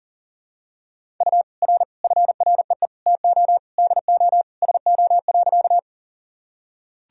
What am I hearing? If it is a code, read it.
Morse code, heard as a keyed tone, about 40 wpm, 700 Hz. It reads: U R FREE TO DO SO.
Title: Sentences from Top 500 Words and Abbreviations 40wpm